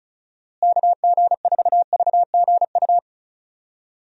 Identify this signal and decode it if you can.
Morse code, heard as a keyed tone, about 34 wpm, 700 Hz. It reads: KG4VGU
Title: USA Call Signs - 6 Characters - Repeated 2x - 35wpm